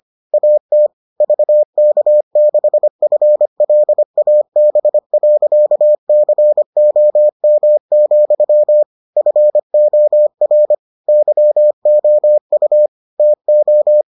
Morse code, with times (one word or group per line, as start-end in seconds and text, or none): 0.3-0.9 AT
1.2-8.8 VK6FLAB.COM,
9.2-10.7 FOR
11.1-12.9 YOU
13.2-14.0 TO